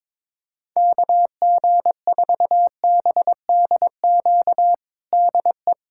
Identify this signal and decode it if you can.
Morse code, heard as a keyed tone, about 22 wpm, 700 Hz. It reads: KG4BDQ DE
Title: USA Callsign DE Callsign 22wpm